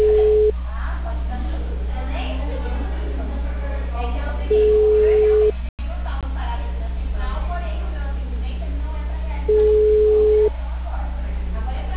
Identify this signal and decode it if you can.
Ringback tone (Europe, 425 Hz), with speech.